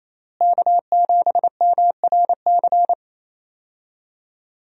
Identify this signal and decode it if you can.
Morse code, heard as a keyed tone, about 28 wpm, 700 Hz. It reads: K7MRC
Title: USA Call Signs - 5 Characters - Repeated 2x - 28wpm